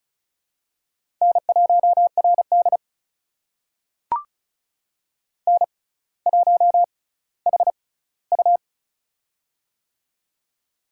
Morse code, heard as a keyed tone, about 36 wpm, spaced as for 18 wpm, 700 Hz. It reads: N1RD N1HU
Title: USA Call Signs - 6x Character Spacing - 35wpm